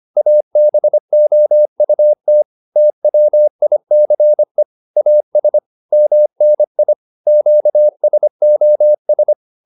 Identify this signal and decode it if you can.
Morse code, heard as a keyed tone, about 25 wpm, 600 Hz. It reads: ABOUT TWICE AS MNI QSOS
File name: SS220 25